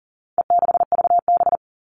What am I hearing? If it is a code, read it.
Morse code, heard as a keyed tone, about 40 wpm, 700 Hz. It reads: E6VB
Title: All Cut Numbers - 4-Digit 40wpm